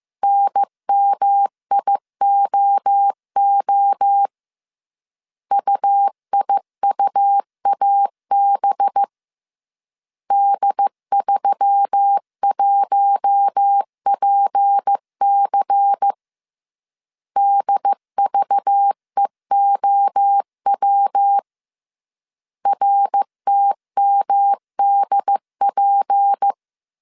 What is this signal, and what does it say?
Morse code, heard as a keyed tone, about 15 wpm, 800 Hz. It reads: NMIOO UIUAB D31PC DVEOW RTMDP